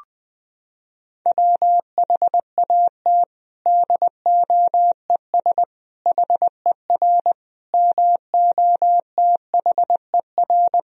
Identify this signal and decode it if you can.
Morse code, heard as a keyed tone, about 20 wpm, 700 Hz. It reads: WHAT DOES HER MOTHER
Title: Sentences from Top 1000 Words - Review Part 4 - 20wpm